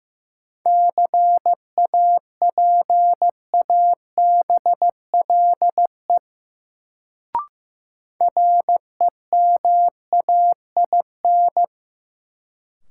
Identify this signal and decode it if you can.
Morse code, heard as a keyed tone, about 15 wpm, 700 Hz. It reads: CAPABLE REMAIN
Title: Sentences from Top 1200 Words - New Word Drill - 15wpm